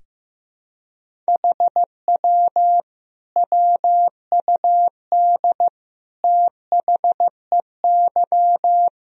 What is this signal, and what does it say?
Morse code, heard as a keyed tone, about 16 wpm, 700 Hz. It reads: HW WUD THEY
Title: Sentences from Top 300 Words and Abbreviations 15wpm